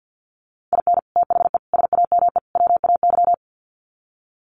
Morse code, heard as a keyed tone, about 50 wpm, 700 Hz. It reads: IN THE SAME WAY
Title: Sets of 4 Words 50wpm